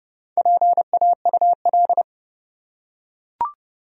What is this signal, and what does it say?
Morse code, heard as a keyed tone, about 30 wpm, 700 Hz. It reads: PAUL
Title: USA Common Names - Max 5 Letters - 30wpm